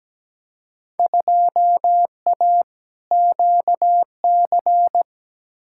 Morse code, heard as a keyed tone, about 17 wpm, 700 Hz. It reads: 2A QC
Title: Field Day Exchanges - 17wpm